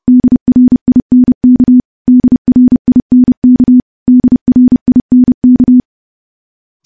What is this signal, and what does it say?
Morse code, heard as a keyed tone, about 30 wpm, 250 Hz. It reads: DRINK DRINK DRINK
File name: morse / morse-training-30WPM